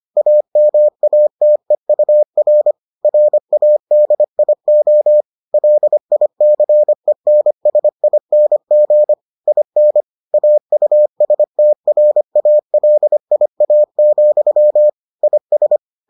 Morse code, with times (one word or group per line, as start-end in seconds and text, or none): 0.2-2.7 AMATEUR
3.0-5.2 RADIO
5.5-9.1 LICENSING
9.5-10.0 IN
10.3-14.9 AUSTRALIA,
15.2-15.8 IS